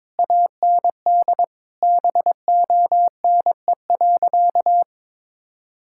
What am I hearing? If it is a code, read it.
Morse code, heard as a keyed tone, about 22 wpm, 700 Hz. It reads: AND BONE.
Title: Sentences from Top 1600 Words 22wpm